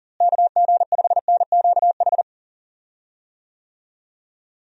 Morse code, heard as a keyed tone, about 40 wpm, 700 Hz. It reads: KG5NQH
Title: USA Call Signs - 6 Characters - Repeated 2x - 40wpm